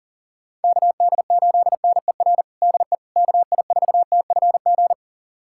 Morse code, heard as a keyed tone, about 40 wpm, 700 Hz. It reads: KD8NER DE KI4TFG
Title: USA Callsign DE Callsign - 6 Characters - 40wpm